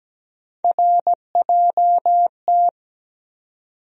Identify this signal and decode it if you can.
Morse code, heard as a keyed tone, about 17 wpm, 700 Hz. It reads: RJT
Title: USA Call Sign Postfixes - 3 Letters 17wpm